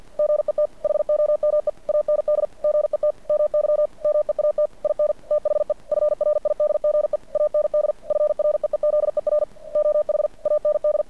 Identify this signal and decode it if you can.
Morse code, heard as a keyed tone, about 50 wpm, 590 Hz. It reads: GET HOME AND GET MY MEAT IN THE FRIDGE AND FREEZER OH AND
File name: archived / CRSnet